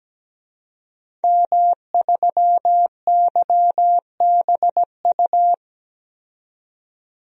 Morse code, heard as a keyed tone, about 17 wpm, 700 Hz. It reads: M3YBU
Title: DXCC-100 Call Signs - 5 Characters - 17wpm